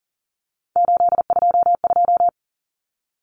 Morse code, heard as a keyed tone, about 40 wpm, 700 Hz. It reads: 822
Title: Common Cut Numbers - 3-Digit 40wpm